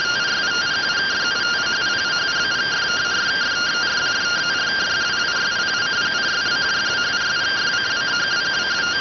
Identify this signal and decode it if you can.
Two-tone teleprinter signal, about 170 Hz shift.